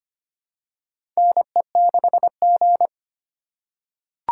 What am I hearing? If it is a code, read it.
Morse code, heard as a keyed tone, about 25 wpm, 700 Hz. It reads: NE6G
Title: USA Call Signs - 4 Characters - 4x Character Spacing - 25wpm